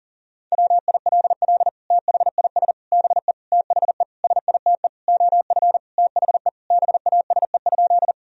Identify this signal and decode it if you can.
Morse code, heard as a keyed tone, about 37 wpm, 700 Hz. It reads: WILL THIS BE THE SITE OF THE BASE?
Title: Sentences from Top 1000 Words - Review Part 1 - 40wpm